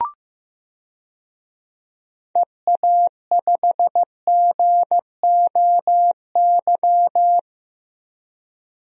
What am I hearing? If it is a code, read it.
Morse code, heard as a keyed tone, about 16 wpm, 700 Hz. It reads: EA5GOY
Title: DXCC-100 Call Signs - 6 Characters - Repeated 3x - 15wpm